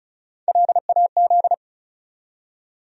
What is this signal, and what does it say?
Morse code, heard as a keyed tone, about 36 wpm, 700 Hz. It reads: LAZ
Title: USA Call Sign Postfixes - 3 Letters 35wpm